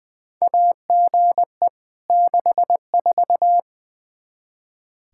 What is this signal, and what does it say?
Morse code, heard as a keyed tone, about 20 wpm, 700 Hz. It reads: AGE 64
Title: Practice QSO - Exchange 3 - More Info - 20wpm